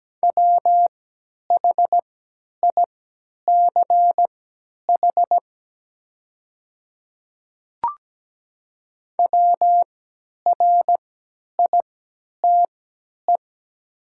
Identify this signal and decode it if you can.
Morse code, heard as a keyed tone, about 17 wpm, spaced as for 10 wpm, 700 Hz. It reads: WHICH WRITE
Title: 3x Character Spacing - Top 100 Words - 17wpm